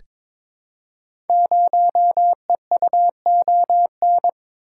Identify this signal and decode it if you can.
Morse code, heard as a keyed tone, about 22 wpm, 700 Hz. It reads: 0EUON
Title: Code Groups - 5 Letter:Numbers - 22wpm